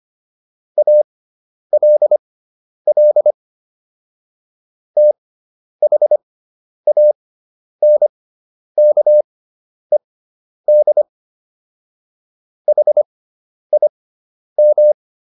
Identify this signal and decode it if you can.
Morse code, heard as a keyed tone, about 25 wpm, spaced as for 10 wpm, 600 Hz. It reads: ALL THANKED HIM